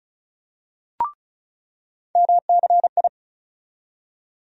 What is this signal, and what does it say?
Morse code, heard as a keyed tone, about 35 wpm, 700 Hz. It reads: MCI
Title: USA Call Sign Postfixes - 3 Letters 35wpm